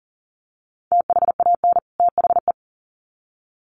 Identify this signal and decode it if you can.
Morse code, heard as a keyed tone, about 40 wpm, 700 Hz. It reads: THAN THE
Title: Sets of 2 Words 40wpm